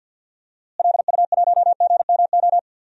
Morse code, heard as a keyed tone, about 48 wpm, 700 Hz. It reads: LU1GMO
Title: DXCC-100 Call Signs - 6 Characters - Repeated 2x - 50wpm